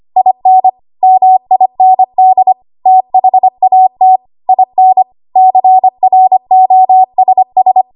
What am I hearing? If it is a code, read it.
Morse code, heard as a keyed tone, about 25 wpm, 750 Hz. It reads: IN MIND THAT IN CROSS
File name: Archive / 25 WPM / 25WPM